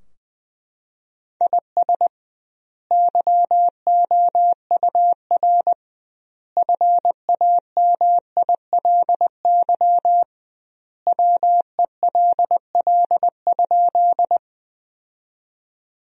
Morse code, heard as a keyed tone, about 20 wpm, 700 Hz. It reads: IS YOUR FAMILY WELL?
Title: 2x Word-Spacing - Sentences from Top 400 Words - 20wpm